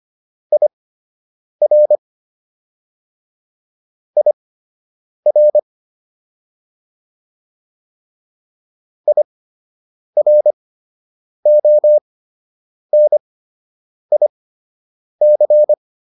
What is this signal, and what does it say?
Morse code, heard as a keyed tone, about 25 wpm, spaced as for 8 wpm, 600 Hz. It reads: IR IR IRONIC